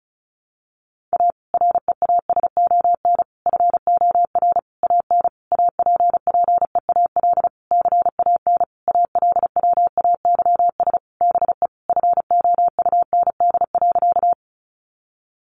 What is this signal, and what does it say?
Morse code, heard as a keyed tone, about 34 wpm, 700 Hz. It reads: A REASON FOR AN APPEAL CAN ALWAYS BE FOUND.
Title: Sentences from Top 1300 Words 35wpm